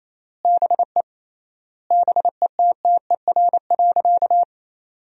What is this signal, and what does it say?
Morse code, heard as a keyed tone, about 28 wpm, 700 Hz. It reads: BE BETTER.
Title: 3x Word-Spacing - Sentences from Top 400 Words - 28wpm